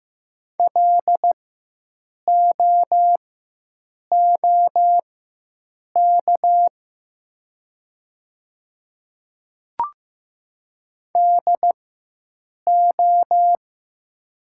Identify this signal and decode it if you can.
Morse code, heard as a keyed tone, about 15 wpm, spaced as for 7 wpm, 700 Hz. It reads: LOOK DO
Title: 4x Character Spacing - Top 100 Words - 15wpm